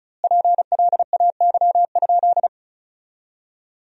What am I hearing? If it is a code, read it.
Morse code, heard as a keyed tone, about 34 wpm, 700 Hz. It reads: PLAY?
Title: 2x Word-Spacing - Sentences from Top 200 Words - 35wpm